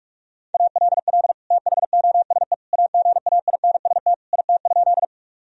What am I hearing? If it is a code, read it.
Morse code, heard as a keyed tone, about 41 wpm, 700 Hz. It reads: ALL THOSE AGAINST IT?
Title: Sentences from Top 1000 Words - Review Part 1 - 45wpm